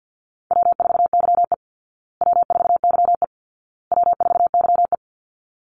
Morse code, heard as a keyed tone, about 50 wpm, 700 Hz. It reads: W4YE W4YE W4YE
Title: Call Signs Repeated 2x - 50wpm